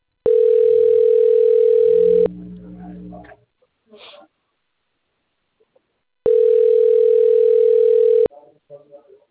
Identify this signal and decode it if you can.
Ringback tone (North America), 440 + 480 Hz, with speech.
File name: rg-1600-unknown- - - .79574